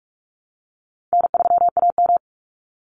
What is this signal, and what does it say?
Morse code, heard as a keyed tone, about 44 wpm, 700 Hz. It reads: N3AM
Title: Easy Call Signs Repeated 3x - 45wpm